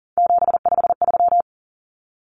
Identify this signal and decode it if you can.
Morse code, heard as a keyed tone, about 40 wpm, 700 Hz. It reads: 753
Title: Common Cut Numbers - 3-Digit 40wpm